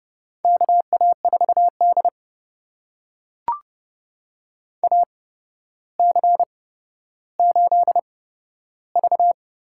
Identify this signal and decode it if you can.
Morse code, heard as a keyed tone, about 31 wpm, spaced as for 13 wpm, 700 Hz. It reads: KA4D AC8V
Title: USA Call Signs - 4 Characters - 8x Character Spacing - 30wpm